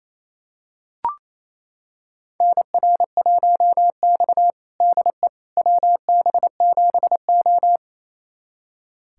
Morse code, heard as a keyed tone, about 29 wpm, 700 Hz. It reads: NR1X DE WB7O
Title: USA Callsign DE Callsign - 4 Characters - 28wpm